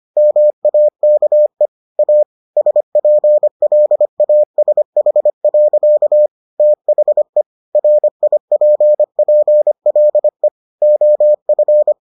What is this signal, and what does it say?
Morse code, heard as a keyed tone, about 25 wpm, 600 Hz. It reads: MAKE A SPLASH. THE RIPPLE OF